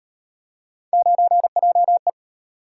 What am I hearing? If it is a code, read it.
Morse code, heard as a keyed tone, about 38 wpm, 700 Hz. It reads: 9JE